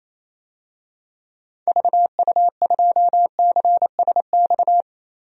Morse code, heard as a keyed tone, about 28 wpm, 700 Hz. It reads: VU2CSX
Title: DXCC-100 Call Signs - 6 Characters - 28wpm